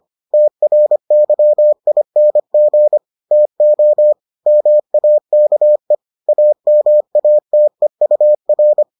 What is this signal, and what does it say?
Morse code, heard as a keyed tone, about 25 wpm, 600 Hz. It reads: TRYING TO MAKE AMATEUR